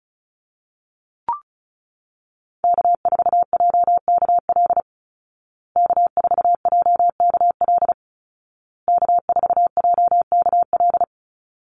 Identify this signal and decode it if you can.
Morse code, heard as a keyed tone, about 34 wpm, 700 Hz. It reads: K4JKL K4JKL K4JKL
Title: Call Signs Repeated 2x - 35wpm